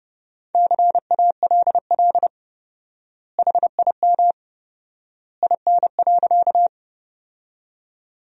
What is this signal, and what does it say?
Morse code, heard as a keyed tone, about 32 wpm, 700 Hz. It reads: CALL HIM IN.
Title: 4x Word-Spacing - Sentences from Top 100 Words - 30wpm